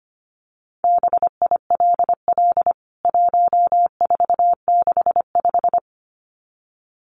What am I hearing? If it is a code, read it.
Morse code, heard as a keyed tone, about 25 wpm, 700 Hz. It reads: BILL 1465
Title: CWT Exchange 25wpm